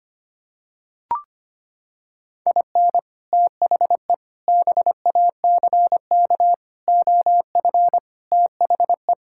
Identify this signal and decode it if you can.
Morse code, heard as a keyed tone, about 25 wpm, 700 Hz. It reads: IN THE BACK OF THE
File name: practice / 25wpm / Sentences with 4-Word Phrases - 25wpm